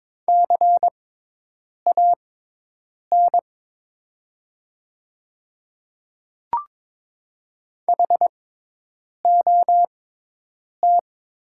Morse code, heard as a keyed tone, about 22 wpm, spaced as for 8 wpm, 700 Hz. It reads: CAN HOT
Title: 6x Character Spacing - Top 100 Words - 22wpm